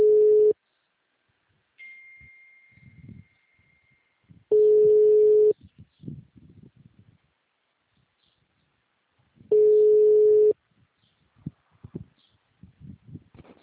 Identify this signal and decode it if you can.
Ringback tone (Europe, 425 Hz).